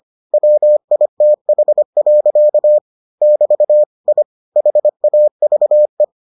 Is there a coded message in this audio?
Yes: Morse code from 0.3 to 6.0 s, heard as a keyed tone, about 25 wpm, 600 Hz. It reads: WITH. <BT> I HAVE